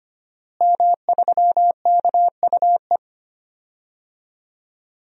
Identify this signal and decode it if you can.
Morse code, heard as a keyed tone, about 25 wpm, 700 Hz. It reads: M3KUE
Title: DXCC-100 Call Signs - 5 Characters - Repeated 2x - 25wpm